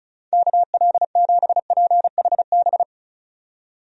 Morse code, heard as a keyed tone, about 36 wpm, 700 Hz. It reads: KL7PHB
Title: DXCC-100 Call Signs - 6 Characters - 35wpm